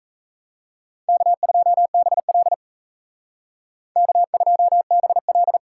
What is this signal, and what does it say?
Morse code, heard as a keyed tone, about 36 wpm, 700 Hz. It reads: K2BL K2BL